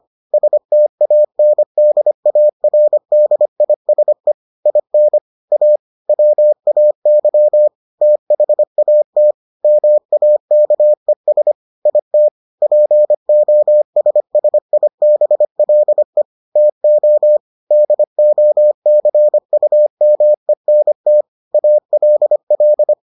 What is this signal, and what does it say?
Morse code, heard as a keyed tone, about 25 wpm, 600 Hz. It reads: STANDARDISE IN A WAY THAT MAKES IT POSSIBLE TO DOCUMENT ALL